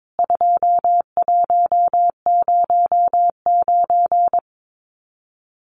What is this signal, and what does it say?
Morse code, heard as a keyed tone, about 22 wpm, 700 Hz. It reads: 2109
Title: CWT Exchange 22wpm